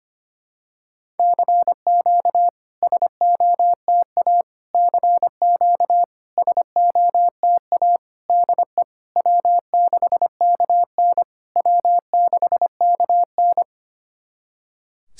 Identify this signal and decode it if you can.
Morse code, heard as a keyed tone, about 25 wpm, 700 Hz. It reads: CQ SOTA CQ SOTA DE W6KN W6KN
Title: SOTA QSO Practice 25wpm